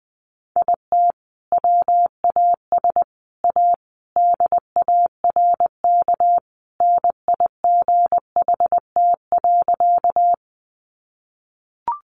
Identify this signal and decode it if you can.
Morse code, heard as a keyed tone, about 20 wpm, 700 Hz. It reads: IT WAS A DARK NIGHT.
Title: Sentences from Top 500 Words 20wpm